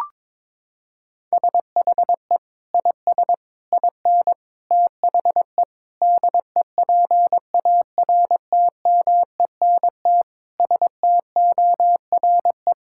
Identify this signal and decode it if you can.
Morse code, heard as a keyed tone, about 22 wpm, 700 Hz. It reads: SHE IS IN THE DEPARTMENT STORE
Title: Sentences from Top 1000 Words - Review Part 5 - 22wpm